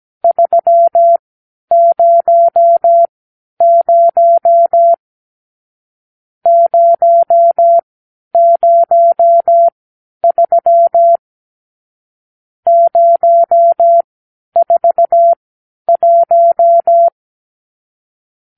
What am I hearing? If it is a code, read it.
Morse code, heard as a keyed tone, about 17 wpm, spaced as for 10 wpm, 680 Hz. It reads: 300 003 041
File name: arrl / Review (1-0)